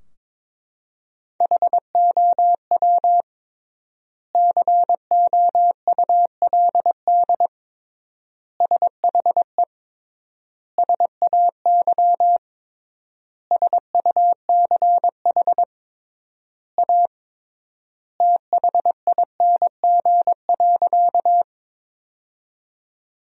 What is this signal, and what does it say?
Morse code, heard as a keyed tone, about 22 wpm, 700 Hz. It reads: HOW COULD SHE SAY SUCH A THING.
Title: 3x Word-Spacing - Sentences from Top 200 Words - 22wpm